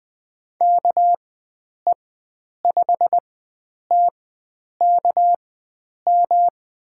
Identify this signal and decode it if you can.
Morse code, heard as a keyed tone, about 20 wpm, spaced as for 9 wpm, 700 Hz. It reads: KE5TKM
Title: USA Call Signs - 6 Characters - 4x Character Spacing - 20wpm